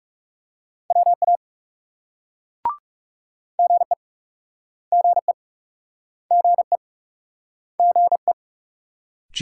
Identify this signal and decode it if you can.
Morse code, heard as a keyed tone, about 39 wpm, 700 Hz. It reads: WA GE GE GE GE